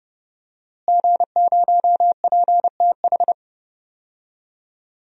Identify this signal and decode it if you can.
Morse code, heard as a keyed tone, about 30 wpm, 700 Hz. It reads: G0PTH
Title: DXCC-100 Call Signs - 5 Characters - Repeated 2x - 30wpm